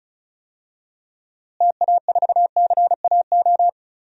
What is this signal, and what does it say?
Morse code, heard as a keyed tone, about 36 wpm, 700 Hz. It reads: TA4CAO
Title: DXCC-100 Call Signs - 6 Characters - Repeated 2x - 35wpm